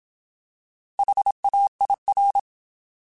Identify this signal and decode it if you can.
Morse code, heard as a keyed tone, about 27 wpm, 800 Hz. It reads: HAIR
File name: Alphabetical / Alphabetical mp3s / Hs @25